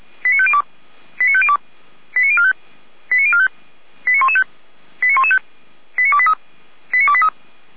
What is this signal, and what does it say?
Five-tone selective call (ZVEI), heard as quick runs of five tones, tones of about 70 ms. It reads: 89502 89502 89045 89045 89115 89115 89283 89283
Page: /455/SDR/ZVEI/89050/